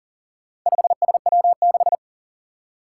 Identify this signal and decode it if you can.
Morse code, heard as a keyed tone, about 40 wpm, 700 Hz. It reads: 5SW6
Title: Four Challenging Letter-Number Combinations 40wpm